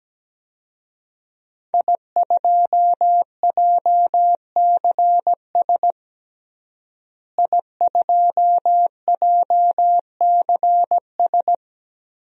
Morse code, heard as a keyed tone, about 17 wpm, 700 Hz. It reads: I2JCS I2JCS
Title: DXCC-100 Call Signs - 5 Characters - Repeated 2x - 17wpm